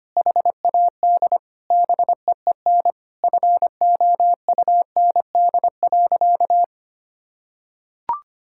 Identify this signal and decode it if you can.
Morse code, heard as a keyed tone, about 25 wpm, 700 Hz. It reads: HAD BEEN FOUND.
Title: Sentences from Top 1600 Words 25wpm